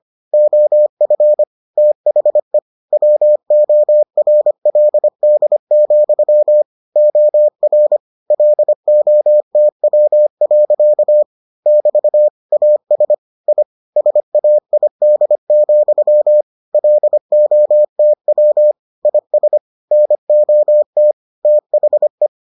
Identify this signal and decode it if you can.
Morse code, heard as a keyed tone, about 25 wpm, 600 Hz. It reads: OF THE WORLD, OR LOTW. <BT> AS I SAID, LOTW IS NOT THE